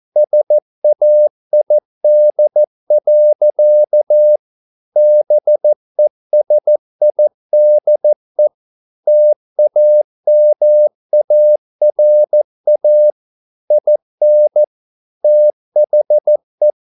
Morse code, heard as a keyed tone, about 14 wpm, 600 Hz. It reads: SAID. BESIDE TAMARA IN THE